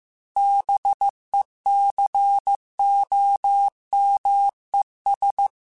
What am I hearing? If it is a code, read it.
Morse code, heard as a keyed tone, about 15 wpm, 800 Hz. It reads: BECOMES
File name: Technical Subjects / Technical Subjects-mp3s / AND Gate @13